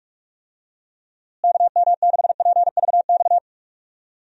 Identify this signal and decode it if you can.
Morse code, heard as a keyed tone, about 42 wpm, 700 Hz. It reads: KM6PVX
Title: USA Call Signs - 6 Characters - Repeated 3x - 45wpm